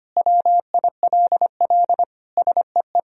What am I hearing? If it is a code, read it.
Morse code, heard as a keyed tone, about 25 wpm, 700 Hz. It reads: WILL SEE
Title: Sentences from Top 1000 Words - Review Part 1 - 25wpm